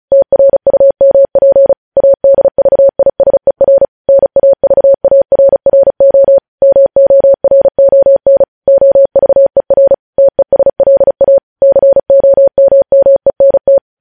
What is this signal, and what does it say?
Morse code, heard as a keyed tone, about 36 wpm, 550 Hz. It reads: TRUMP ADVISER NAVARRO MORON OVER TESLA COMMENT
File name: podcast / News headlines-1x-35x35